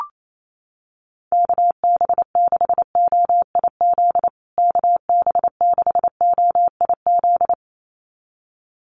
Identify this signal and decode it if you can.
Morse code, heard as a keyed tone, about 28 wpm, 700 Hz. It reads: KB6OIZ KB6OIZ
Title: USA Call Signs Repeated 2x 28wpm